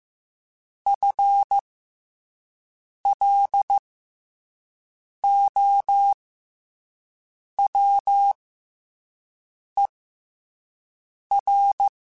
Morse code, heard as a keyed tone, about 15 wpm, spaced as for 5 wpm, 800 Hz. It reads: FLOWER
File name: FL Runs / FL Runs mp3s / FL Runs @5